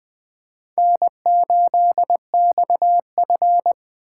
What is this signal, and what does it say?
Morse code, heard as a keyed tone, about 23 wpm, 700 Hz. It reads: N8XF
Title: Speed Racing - USA Call Signs 20wpm